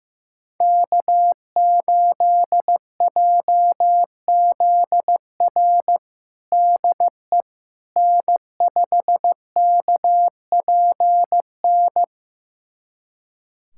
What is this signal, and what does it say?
Morse code, heard as a keyed tone, about 15 wpm, 700 Hz. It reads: K8JZR DE N5KPN
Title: USA Callsign DE Callsign - 5 Characters - 15wpm